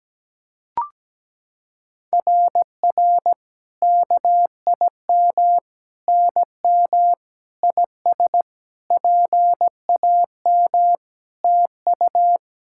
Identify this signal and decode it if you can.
Morse code, heard as a keyed tone, about 17 wpm, 700 Hz. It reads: RR KIM NM IS PAM TU
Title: SOTA QSO Practice 17wpm